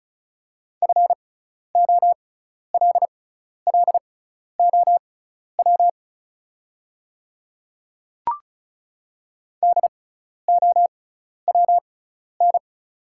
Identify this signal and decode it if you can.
Morse code, heard as a keyed tone, about 36 wpm, spaced as for 12 wpm, 690 Hz. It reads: FOLLOW DOWN
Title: 6x Character Spacing - Top 300 Words - 35wpm